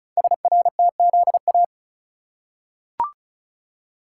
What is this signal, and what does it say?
Morse code, heard as a keyed tone, about 35 wpm, 690 Hz. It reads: SRTZA
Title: Code Groups - 5 Letters - 35wpm